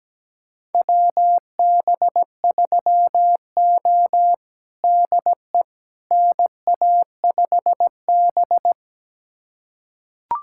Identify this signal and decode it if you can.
Morse code, heard as a keyed tone, about 17 wpm, 700 Hz. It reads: WB3O DE NA5B
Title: USA Callsign DE Callsign - 4 Characters - 17wpm